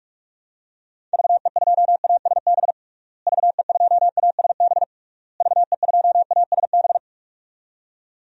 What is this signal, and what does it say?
Morse code, heard as a keyed tone, about 46 wpm, 700 Hz. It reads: VE2ASB VE2ASB VE2ASB
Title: DXCC-100 Call Signs - 6 Characters - Repeated 3x - 45wpm